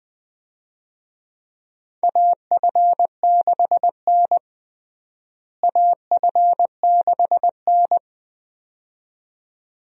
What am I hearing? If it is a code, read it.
Morse code, heard as a keyed tone, about 20 wpm, 700 Hz. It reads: AF6N AF6N
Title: USA Call Signs - 4 Characters - Repeated 2x - 20wpm